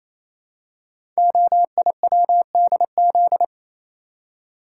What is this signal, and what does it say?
Morse code, heard as a keyed tone, about 27 wpm, 700 Hz. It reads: OIWDZ
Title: Code Groups - 5 Letters - 28wpm